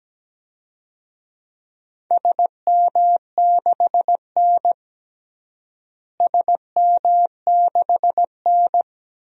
Morse code, heard as a keyed tone, about 17 wpm, 700 Hz. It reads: SM6N SM6N
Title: DXCC-100 Call Signs - 4 Characters - Repeated 2x - 17wpm